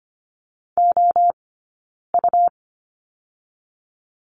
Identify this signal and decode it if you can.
Morse code, heard as a keyed tone, about 25 wpm, 700 Hz. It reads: O U
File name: course / 25:10wpm / 099 Lesson 17d - Letter u - Two Characters 25wpm